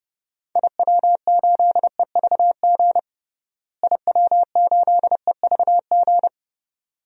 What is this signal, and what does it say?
Morse code, heard as a keyed tone, about 30 wpm, 700 Hz. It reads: IW8EVG IW8EVG
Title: DXCC-100 Call Signs - 6 Characters - Repeated 2x - 30wpm